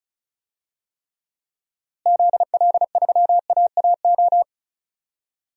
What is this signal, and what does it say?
Morse code, heard as a keyed tone, about 36 wpm, 700 Hz. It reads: ZL3AAO
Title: DXCC-100 Call Signs - 6 Characters - Repeated 2x - 35wpm